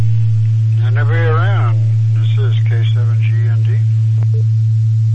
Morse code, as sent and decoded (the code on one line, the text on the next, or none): none